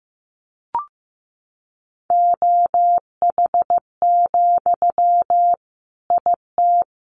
Morse code, heard as a keyed tone, about 15 wpm, 700 Hz. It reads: OH, IT
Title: Sentences from Top 1200 Words 15wpm